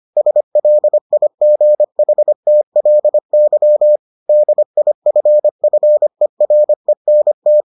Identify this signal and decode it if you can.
Morse code, heard as a keyed tone, about 25 wpm, 600 Hz. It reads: SLIGHTLY DIFFERENT